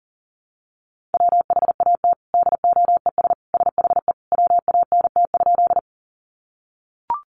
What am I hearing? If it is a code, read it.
Morse code, heard as a keyed tone, about 40 wpm, 700 Hz. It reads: WHAT DOES SHE WANT?
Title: Sentences from Top 200 Words 40wpm